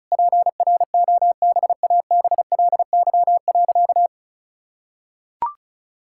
Morse code, heard as a keyed tone, about 34 wpm, 700 Hz. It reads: PROBABLY.
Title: Sentences from Top 1000 Words - Review Part 5 - 35wpm